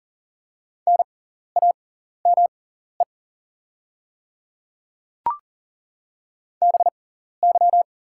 Morse code, heard as a keyed tone, about 40 wpm, spaced as for 14 wpm, 690 Hz. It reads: NAME BY
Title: 6x Character Spacing - Top 200 Words - 40wpm